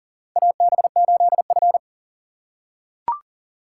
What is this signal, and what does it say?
Morse code, heard as a keyed tone, about 40 wpm, 700 Hz. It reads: AB8F
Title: USA Call Signs - 4 Characters - 2x Character Spacing - 40wpm